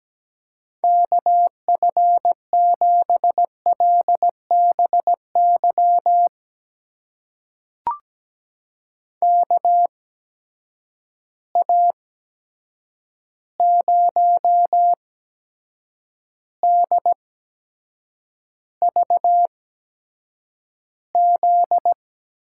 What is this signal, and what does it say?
Morse code, heard as a keyed tone, about 17 wpm, spaced as for 7 wpm, 700 Hz. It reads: KF7LBY KA0DVZ
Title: USA Call Signs - 6 Characters - 8x Character Spacing - 17wpm